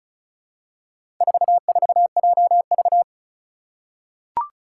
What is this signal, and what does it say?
Morse code, heard as a keyed tone, about 34 wpm, 700 Hz. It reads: 44JV
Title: Four Challenging Letter-Number Combinations 35wpm